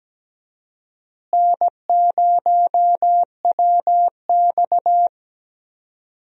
Morse code, heard as a keyed tone, about 17 wpm, 700 Hz. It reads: N0WX
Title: USA Call Signs - 4 Characters - Repeated 3x - 17wpm